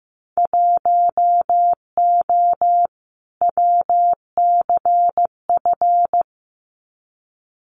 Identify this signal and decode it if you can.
Morse code, heard as a keyed tone, about 15 wpm, 700 Hz. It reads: 1O WCF
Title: Winter Field Day Exchanges 15wpm